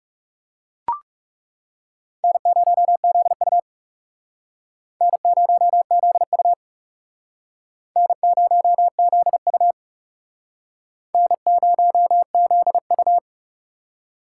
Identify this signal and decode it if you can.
Morse code, heard as a keyed tone, about 41 wpm, 700 Hz. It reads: N0ZU N0ZU N0ZU N0ZU